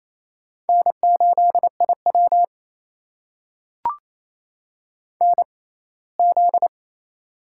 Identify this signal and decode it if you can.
Morse code, heard as a keyed tone, about 28 wpm, spaced as for 19 wpm, 700 Hz. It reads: N8IW NZ
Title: USA Call Signs - 4 Characters - 6x Character Spacing - 28wpm